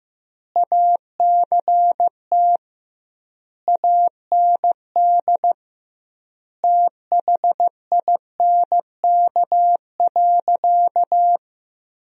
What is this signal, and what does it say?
Morse code, heard as a keyed tone, about 15 wpm, 700 Hz. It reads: ACT AND THINK.
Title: 2x Word-Spacing - Sentences from Top 500 Words - 15wpm